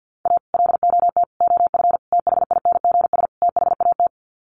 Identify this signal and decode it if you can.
Morse code, heard as a keyed tone, about 50 wpm, 700 Hz. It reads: A LOT OF THINGS THAT
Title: Sets of 5 Words 50wpm